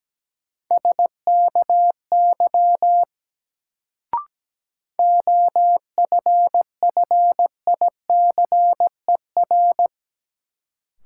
Morse code, heard as a keyed tone, about 17 wpm, 700 Hz. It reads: SKY OFFICER
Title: Sentences from Top 1200 Words - New Word Drill - 17wpm